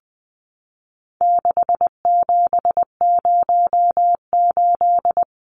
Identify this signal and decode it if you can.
Morse code, heard as a keyed tone, about 20 wpm, 700 Hz. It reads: 6708
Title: Numbers - 4-Digit 20wpm